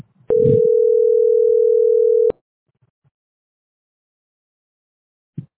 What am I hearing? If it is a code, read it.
Ringback tone (North America), 440 + 480 Hz.